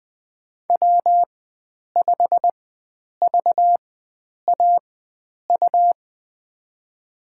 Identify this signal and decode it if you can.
Morse code, heard as a keyed tone, about 20 wpm, spaced as for 9 wpm, 700 Hz. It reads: W5VAU
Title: USA Call Signs - 5 Characters - 4x Character Spacing - 20wpm